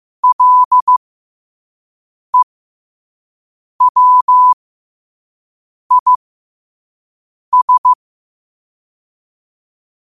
Morse code, heard as a keyed tone, about 15 wpm, spaced as for 5 wpm, 1000 Hz. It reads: LEWIS